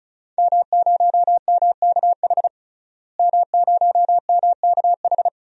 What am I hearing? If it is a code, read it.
Morse code, heard as a keyed tone, about 36 wpm, 700 Hz. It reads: M0MKH M0MKH
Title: DXCC-100 Call Signs - 5 Characters - Repeated 2x - 35wpm